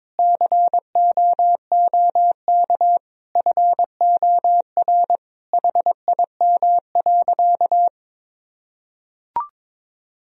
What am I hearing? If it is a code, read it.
Morse code, heard as a keyed tone, about 22 wpm, 700 Hz. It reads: COOK FOR HIM.
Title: Sentences from Top 1000 Words - Review Part 4 - 22wpm